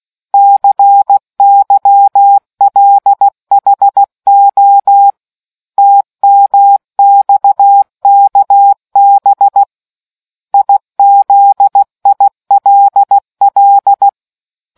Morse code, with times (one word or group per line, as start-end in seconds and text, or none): 0.3-5.1 CYLHO
5.8-9.6 TMXKB
10.5-14.1 IZILL